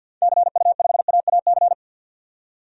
Morse code, heard as a keyed tone, about 52 wpm, 700 Hz. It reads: CU5AAC
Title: DXCC-100 Call Signs - 6 Characters - 50wpm